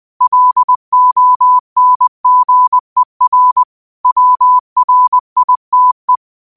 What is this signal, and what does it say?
Morse code, heard as a keyed tone, about 20 wpm, 1000 Hz. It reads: LONGER WRITE
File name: qotdfiles / 20wpm